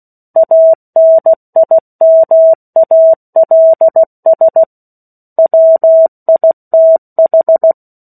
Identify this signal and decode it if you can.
Morse code, heard as a keyed tone, about 16 wpm, 650 Hz. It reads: ANIMALS WITH
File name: 03-Chapter-03 16 WPM